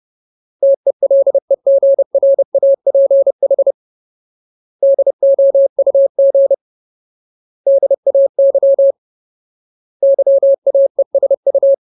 Morse code, heard as a keyed tone, about 30 wpm, 550 Hz. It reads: TELEGRAPH DOUG DAY YAESU
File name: Ultimate Word List / 30WPM Ultimate